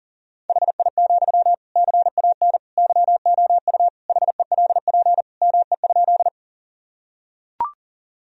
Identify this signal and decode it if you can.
Morse code, heard as a keyed tone, about 40 wpm, 700 Hz. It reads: HI, CAN YOU HELP ME?
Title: Sentences from Top 1000 Words - Review Part 3 - 40wpm